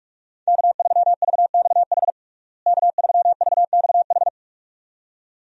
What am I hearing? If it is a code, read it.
Morse code, heard as a keyed tone, about 47 wpm, 700 Hz. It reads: K3VXH K3VXH
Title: USA Call Signs - 5 Characters - Repeated 2x - 45wpm